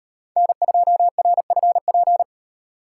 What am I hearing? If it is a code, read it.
Morse code, heard as a keyed tone, about 38 wpm, 700 Hz. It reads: N2RFP